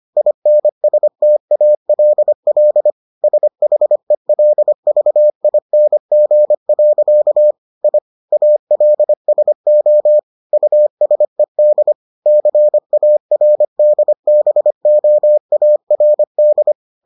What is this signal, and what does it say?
Morse code, heard as a keyed tone, about 25 wpm, 600 Hz. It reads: INSTALL SHELVING. I ALSO USED CARDBOARD